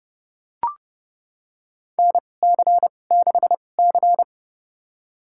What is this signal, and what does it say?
Morse code, heard as a keyed tone, about 30 wpm, spaced as for 22 wpm, 700 Hz. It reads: NC6C
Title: USA Call Signs - 4 Characters - 2x Character Spacing - 30wpm